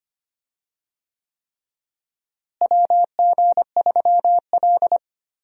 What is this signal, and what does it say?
Morse code, heard as a keyed tone, about 25 wpm, 700 Hz. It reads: WG3L
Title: USA Call Signs - 4 Characters - Repeated 3x - 25wpm